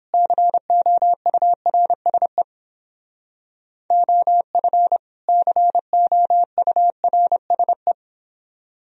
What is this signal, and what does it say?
Morse code, heard as a keyed tone, about 31 wpm, 700 Hz. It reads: COURSE OF COURSE